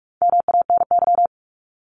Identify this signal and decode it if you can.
Morse code, heard as a keyed tone, about 46 wpm, 700 Hz. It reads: MANY
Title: Top 200 Words 45wpm